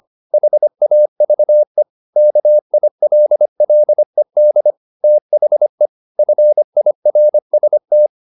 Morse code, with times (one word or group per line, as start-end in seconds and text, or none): 0.3-1.8 HAVE
2.2-4.7 KILLED
5.0-5.9 THE
6.2-8.1 FIRST